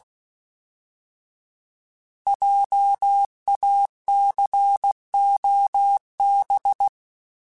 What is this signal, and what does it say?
Morse code, heard as a keyed tone, about 16 wpm, 800 Hz. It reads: JACOB
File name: Js @15